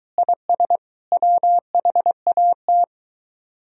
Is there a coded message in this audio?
Yes: Morse code from 0.2 to 2.8 s, heard as a keyed tone, about 23 wpm, 700 Hz. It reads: IS WHAT